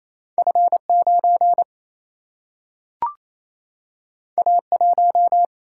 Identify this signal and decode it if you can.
Morse code, heard as a keyed tone, about 28 wpm, 700 Hz. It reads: F 9 A 1
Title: Single-Letter Single-Number 28wpm